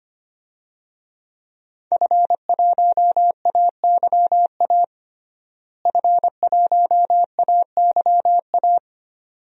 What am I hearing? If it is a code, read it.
Morse code, heard as a keyed tone, about 25 wpm, 700 Hz. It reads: F1AYA F1AYA
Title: DXCC-100 Call Signs - 5 Characters - Repeated 2x - 25wpm